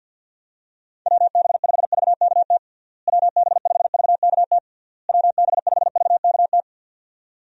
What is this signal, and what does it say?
Morse code, heard as a keyed tone, about 49 wpm, 700 Hz. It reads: WB5VKT WB5VKT WB5VKT
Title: USA Call Signs - 6 Characters - Repeated 3x - 50wpm